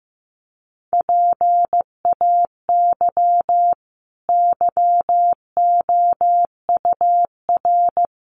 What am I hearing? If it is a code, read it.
Morse code, heard as a keyed tone, about 15 wpm, 700 Hz. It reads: PAY YOUR
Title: Sentences from Top 900 Words 15wpm